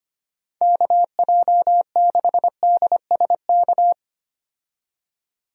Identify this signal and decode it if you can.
Morse code, heard as a keyed tone, about 25 wpm, 700 Hz. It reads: KJ6DSK
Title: USA Call Signs - 6 Characters - 25wpm